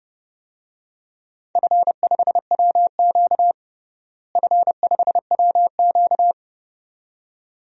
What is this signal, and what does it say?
Morse code, heard as a keyed tone, about 30 wpm, 700 Hz. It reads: F5WQ F5WQ
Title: DXCC-100 Call Signs - 4 Characters - Repeated 2x - 30wpm